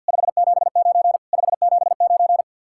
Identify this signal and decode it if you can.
Morse code, heard as a keyed tone, about 60 wpm, 700 Hz. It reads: 579 579
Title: Speed-Racing Common QSO Phrases - 50wpm